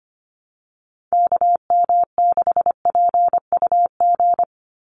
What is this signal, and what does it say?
Morse code, heard as a keyed tone, about 25 wpm, 700 Hz. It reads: KM6PUG
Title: USA Call Signs 25wpm